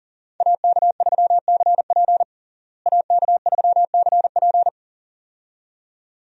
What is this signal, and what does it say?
Morse code, heard as a keyed tone, about 37 wpm, 700 Hz. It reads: AK3CP AK3CP
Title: USA Call Signs - 5 Characters - Repeated 2x - 40wpm